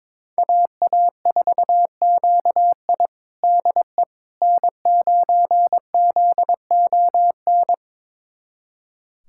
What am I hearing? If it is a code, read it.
Morse code, heard as a keyed tone, about 22 wpm, 700 Hz. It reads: AA4QI DE N9ZON
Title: USA Callsign DE Callsign - 5 Characters - 22wpm